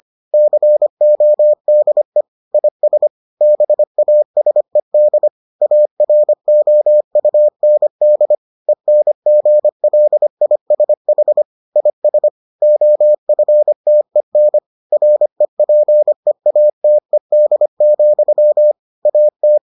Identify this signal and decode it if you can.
Morse code, heard as a keyed tone, about 25 wpm, 600 Hz. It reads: CODE IS BASED AROUND ENGLISH IS OFTEN REPEATED, AT